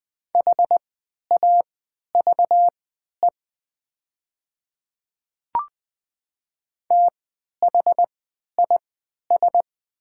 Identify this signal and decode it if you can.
Morse code, heard as a keyed tone, about 20 wpm, spaced as for 11 wpm, 700 Hz. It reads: HAVE THIS
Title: 3x Character Spacing - Top 100 Words - 20wpm